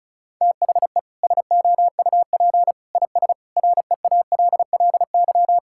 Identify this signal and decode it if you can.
Morse code, heard as a keyed tone, about 34 wpm, 700 Hz. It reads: THE SOUP IS REALLY
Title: Sentences from Top 1000 Words - Review Part 6 - 35wpm